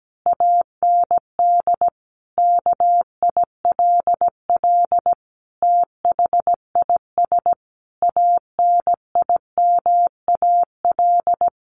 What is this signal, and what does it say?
Morse code, heard as a keyed tone, about 17 wpm, 700 Hz. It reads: AND KILL THIS ANIMAL
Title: A Sound Of Thunder 17wpm